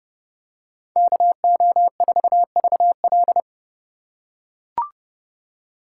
Morse code, heard as a keyed tone, about 31 wpm, 700 Hz. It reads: KO4VL